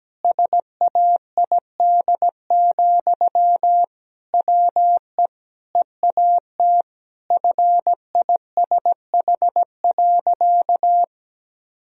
Morse code, heard as a keyed tone, about 17 wpm, 700 Hz. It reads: SAID, WE EAT FISH.
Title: Sentences from Top 1000 Words - Review Part 1 - 17wpm